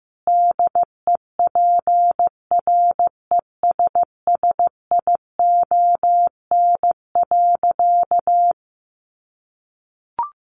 Morse code, heard as a keyed tone, about 15 wpm, 700 Hz. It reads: DEPRESSION.
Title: Sentences from Top 1300 Words 15wpm